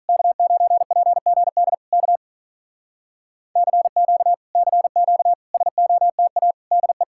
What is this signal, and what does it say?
Morse code, heard as a keyed tone, about 48 wpm, 700 Hz. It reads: K9PGD K CQ CQ SOTA DE